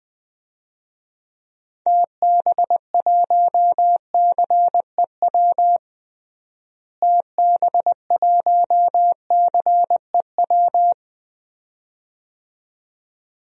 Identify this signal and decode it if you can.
Morse code, heard as a keyed tone, about 20 wpm, 700 Hz. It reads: TB1CEW TB1CEW
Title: DXCC-100 Call Signs - 6 Characters - Repeated 2x - 20wpm